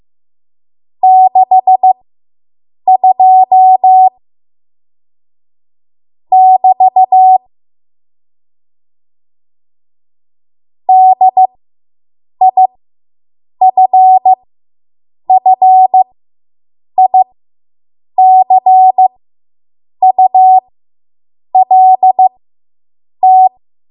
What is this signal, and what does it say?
Morse code, heard as a keyed tone, about 15 wpm, spaced as for 7 wpm, 750 Hz. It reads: 62 <BT> DIFFICULT